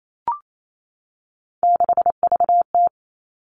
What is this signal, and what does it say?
Morse code, heard as a keyed tone, about 28 wpm, 700 Hz. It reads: 6VT
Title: Three Challenging Letter-Number Combinations 28wpm